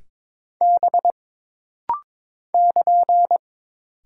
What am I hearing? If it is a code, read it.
Morse code, heard as a keyed tone, about 22 wpm, 700 Hz. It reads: B <KN>
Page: This rapid-fire timing is geared to push your ICR (Instant Character Recognition) skills.